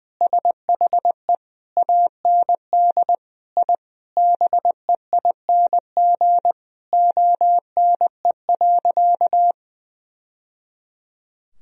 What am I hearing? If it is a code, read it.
Morse code, heard as a keyed tone, about 20 wpm, 700 Hz. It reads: SHE AND I BEING ONE.
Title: Sentences from Top 1000 Words - Review Part 3 - 20wpm